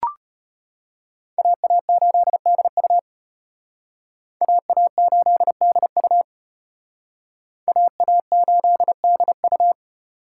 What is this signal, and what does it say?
Morse code, heard as a keyed tone, about 30 wpm, 700 Hz. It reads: AA8DU AA8DU AA8DU
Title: Speed Racing - USA 2x2 Call Signs 25wpm